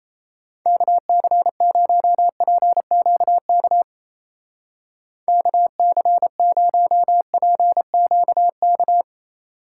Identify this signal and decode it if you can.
Morse code, heard as a keyed tone, about 34 wpm, 700 Hz. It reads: KC0PQK KC0PQK
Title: Speed Racing - USA Call Signs 28wpm